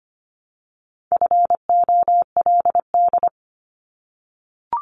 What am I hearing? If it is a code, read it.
Morse code, heard as a keyed tone, about 25 wpm, 700 Hz. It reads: FOLD
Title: Multi-letter Phonemes Repeated 3x 25wpm